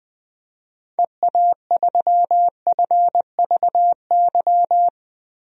Morse code, heard as a keyed tone, about 20 wpm, 700 Hz. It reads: EA3FVY
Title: DXCC-100 Call Signs - 6 Characters - Repeated 3x - 20wpm